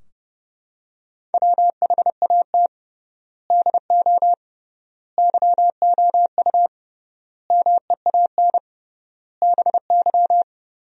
Morse code, heard as a keyed tone, about 32 wpm, 700 Hz. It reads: WHAT DO YOU MEAN BY
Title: 3x Word-Spacing - Sentences from Top 400 Words - 30wpm